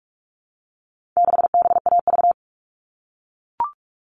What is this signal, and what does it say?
Morse code, heard as a keyed tone, about 44 wpm, 700 Hz. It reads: 6BAV
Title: All Cut Numbers - 4-Digit 45wpm